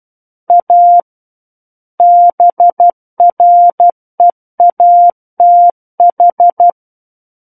Morse code, heard as a keyed tone, about 12 wpm, 700 Hz. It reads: A BREATH
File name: Book One / 03-Chapter 3 - 12 WPM